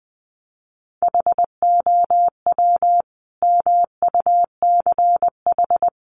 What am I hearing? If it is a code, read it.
Morse code, heard as a keyed tone, about 20 wpm, 700 Hz. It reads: HOW MUCH
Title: Sentences from Top 800 Words 20wpm